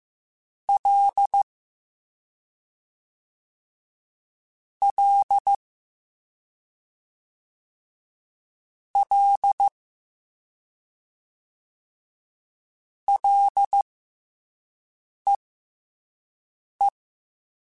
Morse code, heard as a keyed tone, about 15 wpm, spaced as for 5 wpm, 800 Hz. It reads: L L L LEE
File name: F and L / F and L mp3 / F and L Prefixes 3x2 Twice @5